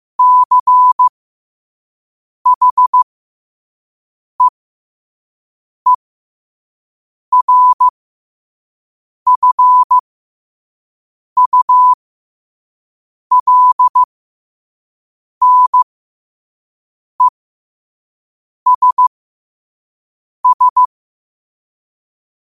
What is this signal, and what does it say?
Morse code, heard as a keyed tone, about 15 wpm, spaced as for 5 wpm, 1000 Hz. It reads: CHEERFULNESS
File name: qotdfiles / 5wpm